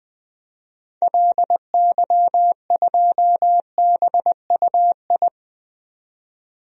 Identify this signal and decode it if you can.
Morse code, heard as a keyed tone, about 20 wpm, 700 Hz. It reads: LY2BUI
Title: DXCC-100 Call Signs - 6 Characters - 20wpm